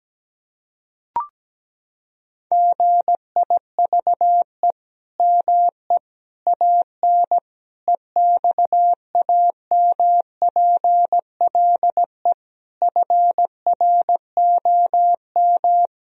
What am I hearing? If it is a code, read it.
Morse code, heard as a keyed tone, about 17 wpm, 700 Hz. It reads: GIVE ME AN EXAMPLE FROM
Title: Sentences from Top 1000 Words - Review Part 3 - 17wpm